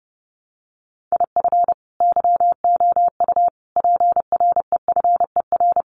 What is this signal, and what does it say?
Morse code, heard as a keyed tone, about 30 wpm, 700 Hz. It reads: IF YOU PREFER
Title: Sentences from Top 1100 Words 30wpm